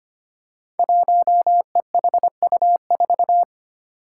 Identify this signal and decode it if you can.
Morse code, heard as a keyed tone, about 25 wpm, 700 Hz. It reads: 1EHU4
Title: Five Challenging Letter-Number Combinations 25wpm